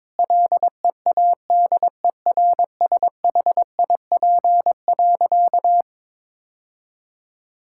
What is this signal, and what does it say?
Morse code, heard as a keyed tone, about 22 wpm, 700 Hz. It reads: LEADERSHIP.
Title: Sentences from Top 1600 Words 22wpm